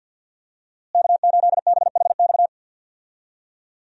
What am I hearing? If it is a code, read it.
Morse code, heard as a keyed tone, about 47 wpm, 700 Hz. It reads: K8BHX
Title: USA Call Signs - 5 Characters - Repeated 3x - 50wpm